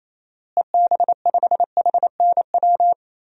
Morse code, heard as a keyed tone, about 28 wpm, 700 Hz. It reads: EB5HNW
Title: DXCC-100 Call Signs - 6 Characters - Repeated 2x - 28wpm